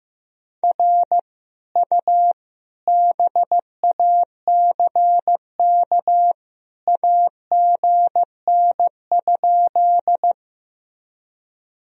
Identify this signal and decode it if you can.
Morse code, heard as a keyed tone, about 16 wpm, 700 Hz. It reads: R U BACK AGN?
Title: Sentences from Top 200 Words and Abbreviations 15wpm